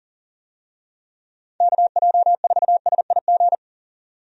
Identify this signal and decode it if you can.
Morse code, heard as a keyed tone, about 39 wpm, 700 Hz. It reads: KJ4SIG
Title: USA Call Signs - 6 Characters - 40wpm